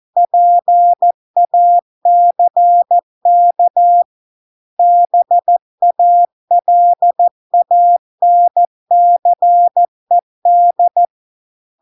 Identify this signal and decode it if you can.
Morse code, heard as a keyed tone, about 14 wpm, 700 Hz. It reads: PACK BALANCED